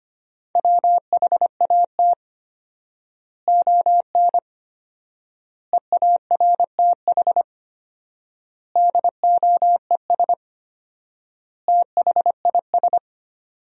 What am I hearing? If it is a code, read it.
Morse code, heard as a keyed tone, about 25 wpm, 700 Hz. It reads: WHAT ON EARTH DOES THIS
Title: 4x Word-Spacing - Sentences from Top 200 Words - 25wpm